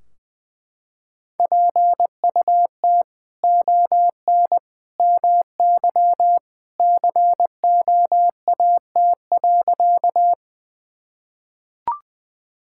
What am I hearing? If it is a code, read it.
Morse code, heard as a keyed tone, about 20 wpm, 700 Hz. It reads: PUT ON MY COAT.
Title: Sentences from Top 1600 Words 20wpm